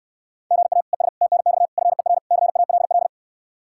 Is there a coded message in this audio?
Yes: Morse code from 0.5 to 3.1 s, heard as a keyed tone, about 68 wpm, 700 Hz. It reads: GA ES TNX FER CALL